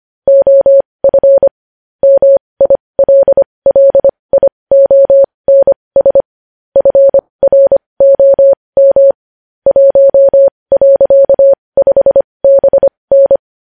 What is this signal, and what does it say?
Morse code, heard as a keyed tone, about 25 wpm, spaced as for 20 wpm, 550 Hz. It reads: OF MILLIONS FROM 1.5BN
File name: podcast / News headlines-1x-25x20